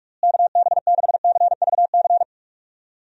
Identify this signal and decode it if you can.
Morse code, heard as a keyed tone, about 44 wpm, 700 Hz. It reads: KB6CVC
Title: USA Call Signs - 6 Characters - Repeated 2x - 45wpm